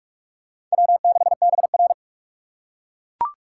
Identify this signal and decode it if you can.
Morse code, heard as a keyed tone, about 48 wpm, 690 Hz. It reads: W6BR
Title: USA Call Signs - 4 Characters - 3x Character Spacing - 45wpm